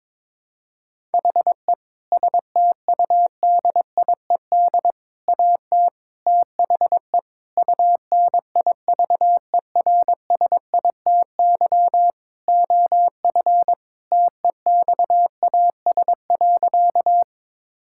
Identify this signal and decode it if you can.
Morse code, heard as a keyed tone, about 22 wpm, 700 Hz. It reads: HE STUDIED AT THE UNIVERSITY OF TEXAS.
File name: practice / 22wpm / Sentences with 4-Word Phrases - 22wpm